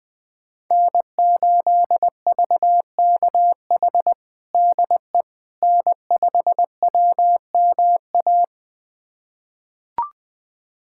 Morse code, heard as a keyed tone, about 21 wpm, 700 Hz. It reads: N8VKH DE N5WMA
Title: USA Callsign DE Callsign - 5 Characters - 20wpm